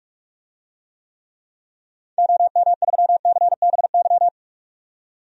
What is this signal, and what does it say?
Morse code, heard as a keyed tone, about 46 wpm, 700 Hz. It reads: OM3CBY
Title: DXCC-100 Call Signs - 6 Characters - 45wpm